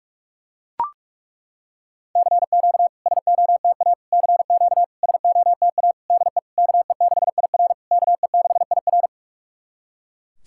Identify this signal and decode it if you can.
Morse code, heard as a keyed tone, about 41 wpm, 690 Hz. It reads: CQ SOTA CQ SOTA DE KE6IR KE6IR
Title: SOTA QSO Practice 45wpm